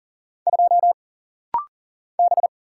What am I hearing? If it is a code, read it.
Morse code, heard as a keyed tone, about 40 wpm, 700 Hz. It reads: 2 B
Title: Single Letter-Number - Rapid-Fire 40wpm